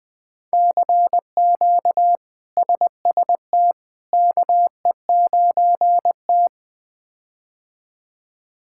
Morse code, heard as a keyed tone, about 20 wpm, 700 Hz. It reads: CQ SST KE9T
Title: K1USN SST QSO Practice - 20wpm